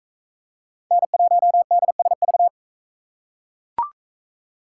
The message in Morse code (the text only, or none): N1DSV